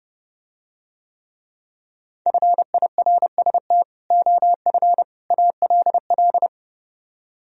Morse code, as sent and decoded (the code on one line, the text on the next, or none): ..-. .. .-. ... - / --- ..-. / .- .-.. .-..
FIRST OF ALL